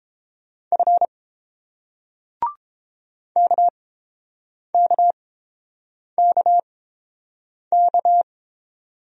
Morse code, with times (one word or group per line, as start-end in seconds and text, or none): 0.7-1.1 F
3.4-8.2 KKKK